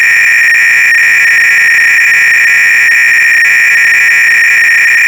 300-baud modem data.
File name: ACR-Audio-Cassette-Programs / CharlesSwiger-Donations / GOBLINS cload-g